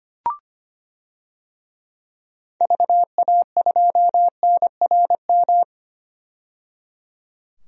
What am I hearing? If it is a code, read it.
Morse code, heard as a keyed tone, about 25 wpm, 700 Hz. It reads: VA2NRM
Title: DXCC-100 Call Signs - 6 Characters - 25wpm